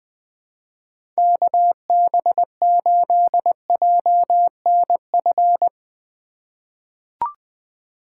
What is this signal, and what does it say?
Morse code, heard as a keyed tone, about 21 wpm, 700 Hz. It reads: KB8JNF
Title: USA Call Signs - 6 Characters - 2x Character Spacing - 20wpm